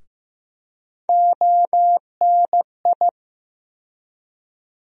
Morse code, heard as a keyed tone, about 15 wpm, 700 Hz. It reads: ONI
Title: USA Call Sign Postfixes - 3 Letters 15wpm